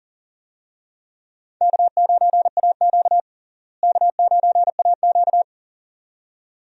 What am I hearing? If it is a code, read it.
Morse code, heard as a keyed tone, about 40 wpm, 700 Hz. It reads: K9AQ K9AQ
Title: USA Call Signs - 4 Characters - Repeated 2x - 40wpm